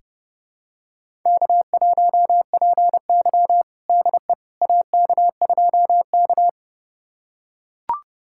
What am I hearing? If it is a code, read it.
Morse code, heard as a keyed tone, about 32 wpm, 700 Hz. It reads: K1PY DE AK2K
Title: USA Callsign DE Callsign - 4 Characters - 30wpm